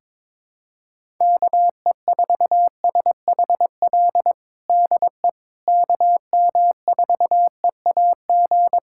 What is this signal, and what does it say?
Morse code, heard as a keyed tone, about 22 wpm, 700 Hz. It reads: KE4SHL DE KM4EAG
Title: USA Callsign DE Callsign - 6 Characters - 22wpm